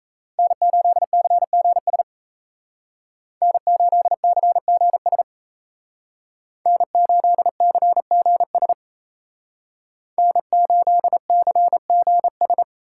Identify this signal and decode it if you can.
Morse code, heard as a keyed tone, about 36 wpm, 700 Hz. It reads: N8CGS N8CGS N8CGS N8CGS